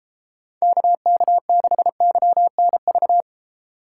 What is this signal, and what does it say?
Morse code, heard as a keyed tone, about 29 wpm, 700 Hz. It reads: KK6YNV